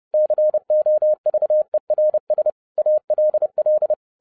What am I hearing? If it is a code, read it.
Morse code, heard as a keyed tone, about 30 wpm, 610 Hz. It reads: COVERS ALL
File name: 2011-07-mp3 / n20 GREETINGS FROM SINGAPORE 30wpm